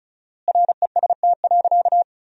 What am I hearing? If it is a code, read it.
Morse code, heard as a keyed tone, about 34 wpm, 700 Hz. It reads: REST.
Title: 4x Word-Spacing - Sentences from Top 500 Words - 35wpm